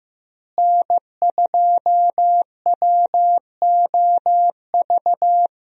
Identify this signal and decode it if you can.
Morse code, heard as a keyed tone, about 15 wpm, 700 Hz. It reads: N2WOV
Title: USA Call Signs - 5 Characters - Repeated 2x - 15wpm